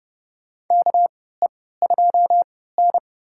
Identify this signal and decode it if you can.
Morse code, heard as a keyed tone, about 30 wpm, spaced as for 17 wpm, 700 Hz. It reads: KE2N
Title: USA Call Signs - 4 Characters - 3x Character Spacing - 30wpm